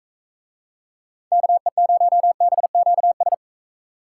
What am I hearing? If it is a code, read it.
Morse code, heard as a keyed tone, about 40 wpm, 700 Hz. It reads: KE0BQS